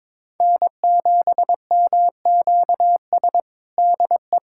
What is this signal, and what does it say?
Morse code, heard as a keyed tone, about 22 wpm, 700 Hz. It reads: N7MQS DE
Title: USA Callsign DE Callsign 22wpm